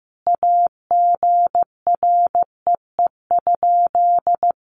Morse code, heard as a keyed tone, about 15 wpm, 700 Hz. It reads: AGREE?
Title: Sentences from Top 1100 Words 15wpm